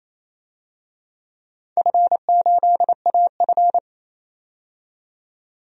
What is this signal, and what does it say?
Morse code, heard as a keyed tone, about 27 wpm, 700 Hz. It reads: F8AF
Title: DXCC-100 Call Signs - 4 Characters - 28wpm